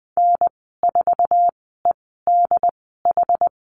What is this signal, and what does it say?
Morse code, heard as a keyed tone, about 20 wpm, spaced as for 15 wpm, 700 Hz. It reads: N4EDH
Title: 088 Lesson 15g - Number 5 - Call Signs 20wpm